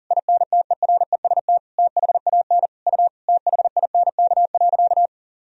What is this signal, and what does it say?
Morse code, heard as a keyed tone, about 40 wpm, 700 Hz. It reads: INTEREST THAN U THINK.
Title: Sentences from Top 400 Words and Abbreviations 40wpm